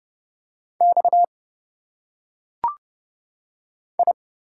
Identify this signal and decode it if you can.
Morse code, heard as a keyed tone, about 30 wpm, 700 Hz. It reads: X I
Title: Single Letters - Phonetic Pronunciation 30wpm